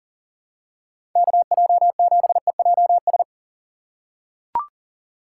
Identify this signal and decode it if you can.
Morse code, heard as a keyed tone, about 40 wpm, 700 Hz. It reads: KJ7EJS
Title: USA Call Signs - 6 Characters - 6x Character Spacing - 40wpm